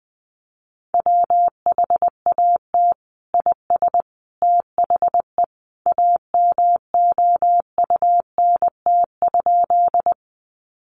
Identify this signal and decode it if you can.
Morse code, heard as a keyed tone, about 20 wpm, 700 Hz. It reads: WHAT IS THE AMOUNT?
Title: Sentences from Top 700 Words 20wpm